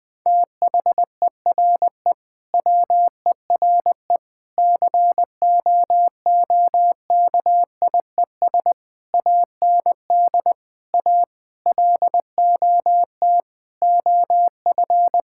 Morse code, heard as a keyed tone, about 21 wpm, 700 Hz. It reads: THERE WERE COOKIES AND A LOT OF
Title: Sentences with 4-Word Phrases - 20wpm